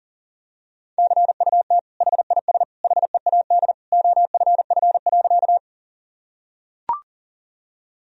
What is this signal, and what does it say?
Morse code, heard as a keyed tone, about 40 wpm, 700 Hz. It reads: CUT HIS HEAD OFF.
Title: Sentences from Top 1000 Words - Review Part 1 - 40wpm